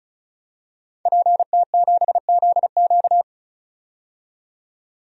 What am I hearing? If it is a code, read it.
Morse code, heard as a keyed tone, about 36 wpm, 700 Hz. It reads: PT7ZQ
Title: DXCC-100 Call Signs - 5 Characters - 35wpm